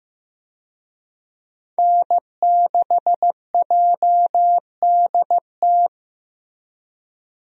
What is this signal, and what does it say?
Morse code, heard as a keyed tone, about 15 wpm, 700 Hz. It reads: N6JDT
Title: USA Call Signs - 5 Characters - Repeated 3x - 15wpm